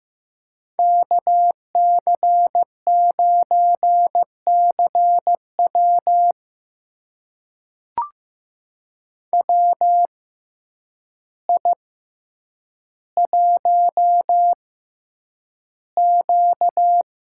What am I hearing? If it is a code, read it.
Morse code, heard as a keyed tone, about 15 wpm, spaced as for 8 wpm, 700 Hz. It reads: KC9CW WI1Q
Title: USA Call Signs - 6x Character Spacing - 15wpm